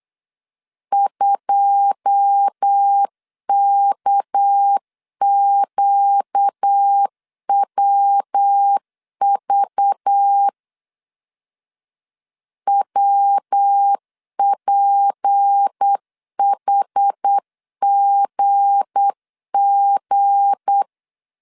Morse code, heard as a keyed tone, about 8 wpm, 800 Hz. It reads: 2KQWV WPHGG
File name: CW / DET / 1 9 1500C S C 8wpm 5653